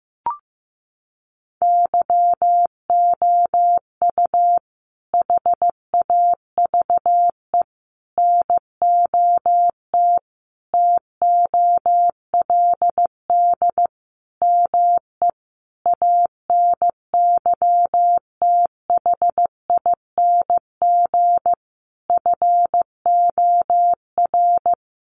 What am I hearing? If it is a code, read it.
Morse code, heard as a keyed tone, about 15 wpm, 700 Hz. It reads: YOU HAVE NOT TOLD ME ANYTHING FOR
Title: Sentences from Top 1000 Words 15wpm